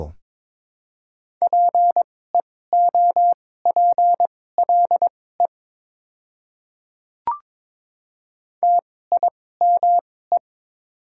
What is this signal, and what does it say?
Morse code, heard as a keyed tone, about 21 wpm, spaced as for 16 wpm, 700 Hz. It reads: PEOPLE TIME
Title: 2x Character Spacing - Top 200 Words - 22wpm